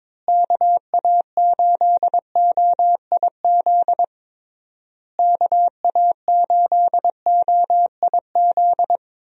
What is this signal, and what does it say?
Morse code, heard as a keyed tone, about 22 wpm, 700 Hz. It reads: KA8OIZ KA8OIZ
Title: USA Call Signs - 6 Characters - Repeated 3x - 22wpm